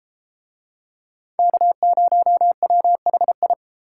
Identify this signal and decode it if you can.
Morse code, heard as a keyed tone, about 31 wpm, 700 Hz. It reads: K0WHI